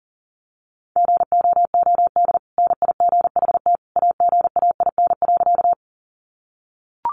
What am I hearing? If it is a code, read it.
Morse code, heard as a keyed tone, about 40 wpm, 700 Hz. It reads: GOOD NIGHT AGAIN.
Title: Sentences from Top 300 Words 40wpm